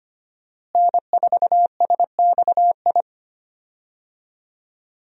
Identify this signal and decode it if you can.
Morse code, heard as a keyed tone, about 25 wpm, 700 Hz. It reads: N4SXI
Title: USA Call Signs - 5 Characters - Repeated 3x - 25wpm